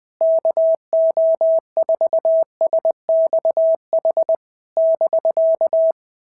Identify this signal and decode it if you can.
Morse code, heard as a keyed tone, about 20 wpm, 650 Hz. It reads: KO4SXH <BK>
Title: POTA QSO Practice - 20wpm